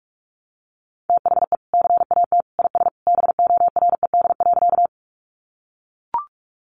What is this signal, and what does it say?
Morse code, heard as a keyed tone, about 44 wpm, 700 Hz. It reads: THE CAT IS BORED.
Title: Easy Phrases 45wpm